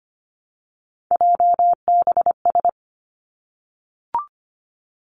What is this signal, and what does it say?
Morse code, heard as a keyed tone, about 25 wpm, 700 Hz. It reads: JBS
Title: Three Challenging Letter-Number Combinations 25wpm